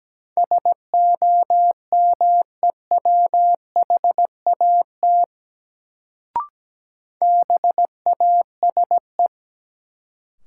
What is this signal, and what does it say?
Morse code, heard as a keyed tone, about 17 wpm, 700 Hz. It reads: SOMEWHAT BASE
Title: Sentences from Top 1000 Words - Review Part 1 - New Word Drill - 17wpm